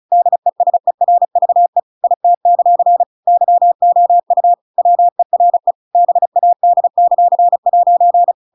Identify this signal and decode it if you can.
Morse code, heard as a keyed tone, about 34 wpm, 700 Hz. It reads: DESERVE IT; YOU WERE BAD;'
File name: Pinocchio / Pinocchio ch7 35 35 ik1hge